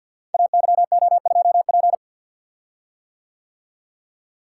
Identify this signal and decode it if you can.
Morse code, heard as a keyed tone, about 50 wpm, 700 Hz. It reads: AYO2P
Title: Code Groups - 5 Letter:Numbers - 50wpm